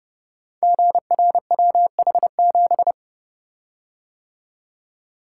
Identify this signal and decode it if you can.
Morse code, heard as a keyed tone, about 30 wpm, 700 Hz. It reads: GRWH7
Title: Code Groups - 5 Letter:Numbers - 30wpm